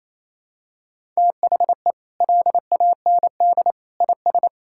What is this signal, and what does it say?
Morse code, heard as a keyed tone, about 28 wpm, 700 Hz. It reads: THE LAND IS
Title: Sentences from Top 1000 Words - Review Part 6 - 28wpm